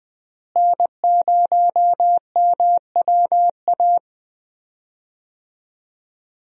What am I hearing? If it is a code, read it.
Morse code, heard as a keyed tone, about 21 wpm, 700 Hz. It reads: N0MWA
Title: USA Call Signs - 5 Characters - Repeated 2x - 20wpm